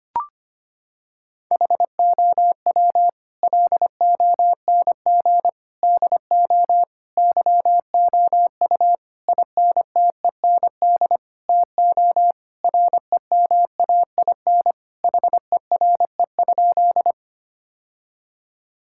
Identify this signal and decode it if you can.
Morse code, heard as a keyed tone, about 25 wpm, 700 Hz. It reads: HOW LONG DO YOU INTEND TO REMAIN HERE?
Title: Sentences from Top 1600 Words 25wpm